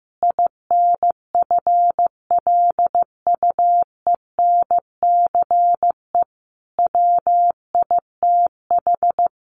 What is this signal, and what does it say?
Morse code, heard as a keyed tone, about 14 wpm, 700 Hz. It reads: INFLUENCE WITH
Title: Sentences from Top 1100 Words 15wpm